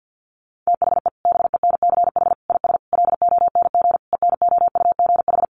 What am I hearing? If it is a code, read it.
Morse code, heard as a keyed tone, about 50 wpm, 700 Hz. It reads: THE BENCH IS LONG ENOUGH
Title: Sentences from Top 1400 Words 50wpm